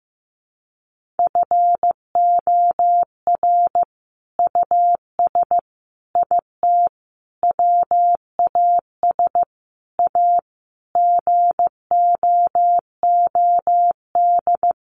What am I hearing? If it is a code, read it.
Morse code, heard as a keyed tone, about 15 wpm, 700 Hz. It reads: FOR US IT WAS A GOOD
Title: Sentences from Top 1200 Words 15wpm